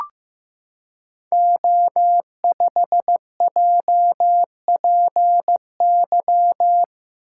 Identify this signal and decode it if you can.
Morse code, heard as a keyed tone, about 15 wpm, 700 Hz. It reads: O5JPY
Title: Code Groups - 5 Letter:Numbers - 15wpm